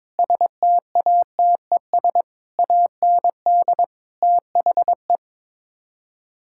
Morse code, heard as a keyed tone, about 22 wpm, 700 Hz. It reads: STATES AND THE
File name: practice / 22wpm / Build A Phrase - 5-Word Phrases - 22wpm